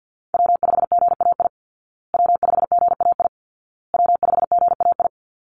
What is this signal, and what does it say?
Morse code, heard as a keyed tone, about 48 wpm, 700 Hz. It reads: W5GAI W5GAI W5GAI
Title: Call Signs Repeated 2x - 50wpm